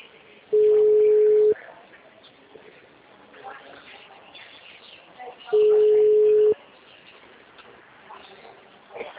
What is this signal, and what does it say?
Ringback tone (Europe, 425 Hz), with speech.